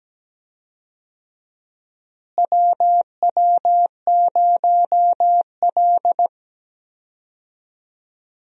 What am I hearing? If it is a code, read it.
Morse code, heard as a keyed tone, about 17 wpm, 700 Hz. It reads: WW0L
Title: USA Call Signs - 4 Characters - Repeated 2x - 17wpm